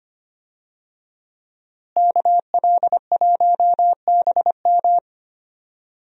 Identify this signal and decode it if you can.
Morse code, heard as a keyed tone, about 25 wpm, 700 Hz. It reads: KL1BM
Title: USA Call Signs - 5 Characters - Repeated 3x - 25wpm